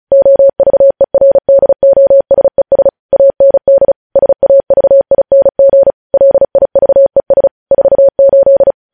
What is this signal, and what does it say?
Morse code, heard as a keyed tone, about 36 wpm, 550 Hz. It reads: OVERDOSES AND SAVING LIVES 48
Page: News headlines in Morse code at 35 wpm